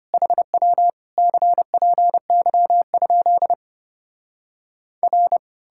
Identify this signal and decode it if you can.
Morse code, heard as a keyed tone, about 33 wpm, 700 Hz. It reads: HW CPY? R